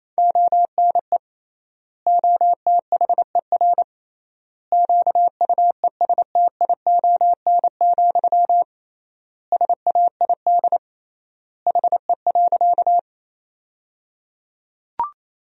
Morse code, heard as a keyed tone, about 28 wpm, 700 Hz. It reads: ONE OTHER QUESTION, SAID HE.
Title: 3x Word-Spacing - Sentences from Top 400 Words - 28wpm